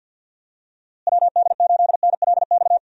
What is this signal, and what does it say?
Morse code, heard as a keyed tone, about 48 wpm, 700 Hz. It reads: WD8NLX
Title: USA Call Signs - 6 Characters - Repeated 2x - 50wpm